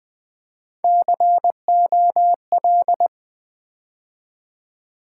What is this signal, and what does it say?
Morse code, heard as a keyed tone, about 20 wpm, 700 Hz. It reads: COL
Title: Build A Word - 3 Letters At A Time - 9-Letter Words - 20wpm